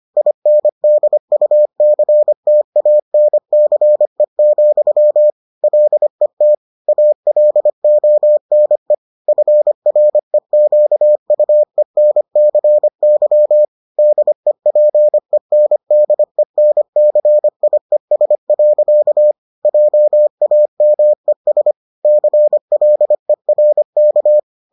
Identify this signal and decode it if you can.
Morse code, heard as a keyed tone, about 25 wpm, 600 Hz. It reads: INDUCTANCE, LET ALONE FREQUENCY DEPENDENCIES. JAMES CLERK